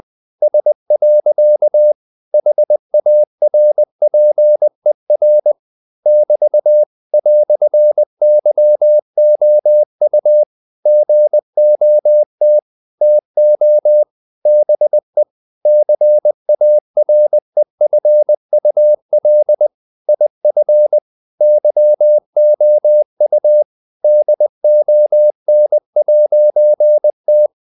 Morse code, heard as a keyed tone, about 20 wpm, 600 Hz. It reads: S. HARPER <BT> "YOU GOT TO BE CAREFUL IF YOU DON'T
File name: CW / Practice / EnglishQuotations0320